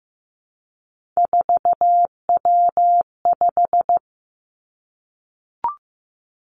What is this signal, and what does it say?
Morse code, heard as a keyed tone, about 15 wpm, 700 Hz. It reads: 4W5
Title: Three Challenging Letter-Number Combinations 15wpm